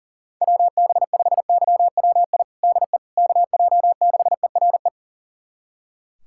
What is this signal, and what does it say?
Morse code, heard as a keyed tone, about 40 wpm, 700 Hz. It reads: WB5YWI DE KJ6ERE
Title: USA Callsign DE Callsign - 6 Characters - 40wpm